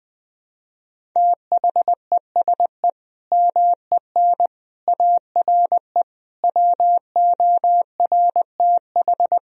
Morse code, heard as a keyed tone, about 21 wpm, 700 Hz. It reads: THESE MEN ARE WORTH
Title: Sentences from Top 1000 Words - Review Part 1 - 20wpm